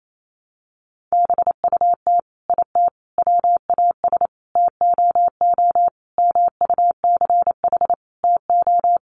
Morse code, heard as a keyed tone, about 28 wpm, 700 Hz. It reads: BUT IT WAS TOO MUCH TO
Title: Sentences from Top 1100 Words 28wpm